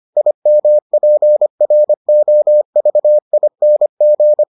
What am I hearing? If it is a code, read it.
Morse code, heard as a keyed tone, about 25 wpm, 600 Hz. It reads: IMPROVING